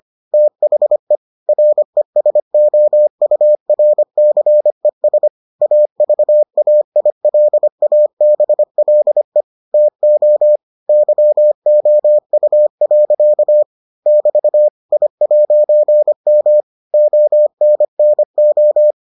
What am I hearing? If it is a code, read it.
Morse code, heard as a keyed tone, about 25 wpm, 600 Hz. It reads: THE RESOURCES AVAILABLE TO YOU. <BT> I'M ONNO